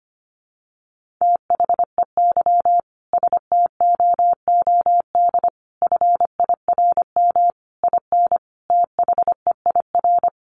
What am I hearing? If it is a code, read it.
Morse code, heard as a keyed tone, about 25 wpm, 700 Hz. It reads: THEY STOOD FIRM IN THEIR
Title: Sentences from Top 1400 Words 25wpm